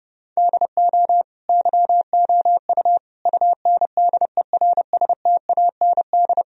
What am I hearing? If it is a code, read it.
Morse code, heard as a keyed tone, about 30 wpm, 700 Hz. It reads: DO YOU UNDERSTAND
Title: Sentences from Top 1000 Words - Review Part 3 - 30wpm